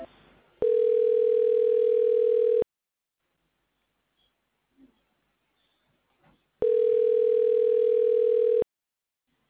Ringback tone (North America), 440 + 480 Hz.